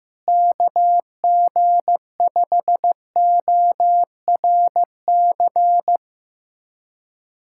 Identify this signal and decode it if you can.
Morse code, heard as a keyed tone, about 16 wpm, 700 Hz. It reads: KG5ORC
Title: USA Callsign DE Callsign 15wpm